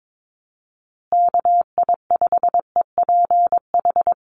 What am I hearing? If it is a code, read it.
Morse code, heard as a keyed tone, about 22 wpm, 700 Hz. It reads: KI5EPH
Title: USA Call Signs Repeated 2x 22wpm